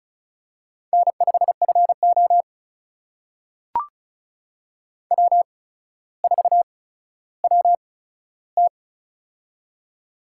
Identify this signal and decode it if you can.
Morse code, heard as a keyed tone, about 36 wpm, spaced as for 15 wpm, 700 Hz. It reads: N5FO W4WT
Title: USA Call Signs - 4 Characters - 8x Character Spacing - 35wpm